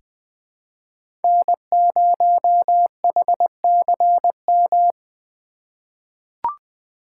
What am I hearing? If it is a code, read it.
Morse code, heard as a keyed tone, about 20 wpm, 700 Hz. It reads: N0HCM
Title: USA Call Signs - 5 Characters - 4x Character Spacing - 20wpm